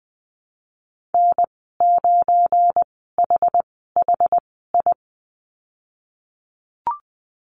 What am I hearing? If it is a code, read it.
Morse code, heard as a keyed tone, about 20 wpm, spaced as for 15 wpm, 700 Hz. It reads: N9HHI
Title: 179 Lesson 29g - Slash - Call Signs 20wpm